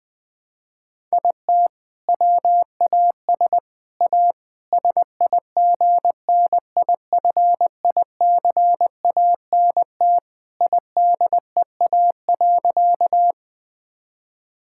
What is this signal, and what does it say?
Morse code, heard as a keyed tone, about 20 wpm, 700 Hz. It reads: IT WAS A SIGNIFICANT IDEA.
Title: Sentences from Top 1000 Words - Review Part 6 - 20wpm